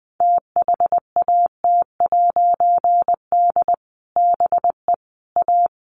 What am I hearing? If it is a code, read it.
Morse code, heard as a keyed tone, about 20 wpm, 700 Hz. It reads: THAT'D BE A
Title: A Sound Of Thunder 20wpm